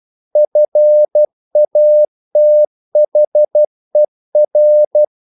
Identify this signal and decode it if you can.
Morse code, heard as a keyed tone, about 12 wpm, 600 Hz. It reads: FATHER
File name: mp3 files / the gardener - 12wpm